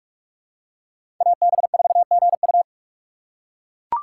Morse code, heard as a keyed tone, about 41 wpm, 690 Hz. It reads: AB4GU
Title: USA Call Signs - 5 Characters - 2x Character Spacing - 45wpm